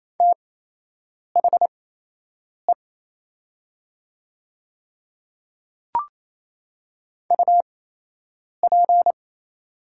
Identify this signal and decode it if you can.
Morse code, heard as a keyed tone, about 28 wpm, spaced as for 8 wpm, 700 Hz. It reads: THE UP
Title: 8x Character Spacing - Top 100 Words - 28wpm